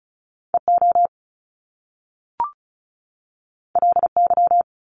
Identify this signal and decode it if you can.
Morse code, heard as a keyed tone, about 36 wpm, 700 Hz. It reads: EO LY
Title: Top Two Letter Combinations in Words 35wpm